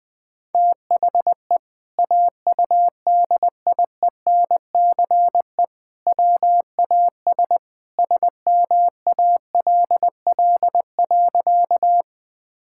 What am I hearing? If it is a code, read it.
Morse code, heard as a keyed tone, about 20 wpm, 700 Hz. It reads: THE AUDIENCE WAS SMALL.
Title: Sentences from Top 1000 Words - Review Part 5 - 20wpm